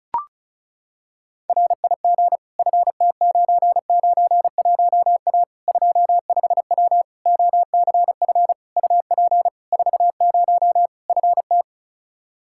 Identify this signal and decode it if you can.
Morse code, heard as a keyed tone, about 36 wpm, 700 Hz. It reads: RIG FT991A 25W OCF UP 40 FT
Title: Practice QSO - Exchange 2 - Info - 35wpm